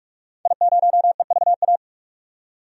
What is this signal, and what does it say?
Morse code, heard as a keyed tone, about 41 wpm, 700 Hz. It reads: I0EVA
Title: DXCC-100 Call Signs - 5 Characters - 45wpm